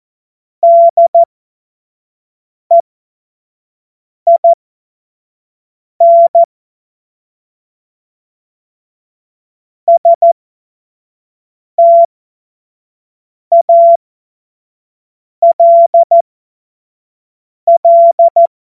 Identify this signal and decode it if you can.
Morse code, heard as a keyed tone, about 14 wpm, spaced as for 5 wpm, 680 Hz. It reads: DEIN STALL
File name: Witz-26-August-2025-25-F70